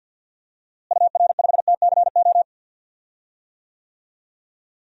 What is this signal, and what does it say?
Morse code, heard as a keyed tone, about 53 wpm, 700 Hz. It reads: UR5TCO
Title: DXCC-100 Call Signs - 6 Characters - Repeated 2x - 50wpm